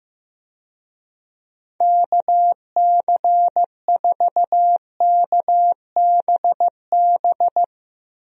Morse code, heard as a keyed tone, about 15 wpm, 700 Hz. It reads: KC4KBB
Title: USA Call Signs - 6 Characters - Repeated 3x - 15wpm